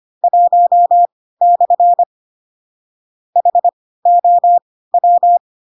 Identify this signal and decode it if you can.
Morse code, heard as a keyed tone, about 25 wpm, spaced as for 16 wpm, 700 Hz. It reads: 1/ HOW
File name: Pinocchio / Pinocchio ch1 10 25 ik1hge